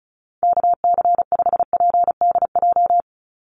Morse code, heard as a keyed tone, about 34 wpm, 700 Hz. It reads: KC5PDJ
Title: USA Call Signs 35wpm